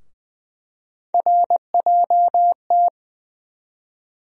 Morse code, heard as a keyed tone, about 20 wpm, 700 Hz. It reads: RJT
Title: USA Call Sign Postfixes - 3 Letters 20wpm